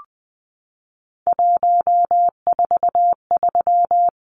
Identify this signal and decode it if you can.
Morse code, heard as a keyed tone, about 20 wpm, 700 Hz. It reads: 143
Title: Numbers - 3-Digit 20wpm